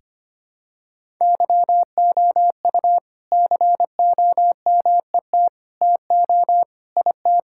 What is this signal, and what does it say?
Morse code, heard as a keyed tone, about 25 wpm, 700 Hz. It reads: YOU COMET TO IT
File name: practice / 25wpm / All-Age Appropriate Jokes - 25wpm